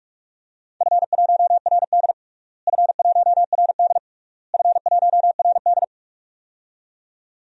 Morse code, heard as a keyed tone, about 46 wpm, 700 Hz. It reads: F1RD F1RD F1RD
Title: DXCC-100 Call Signs - 4 Characters - Repeated 3x - 45wpm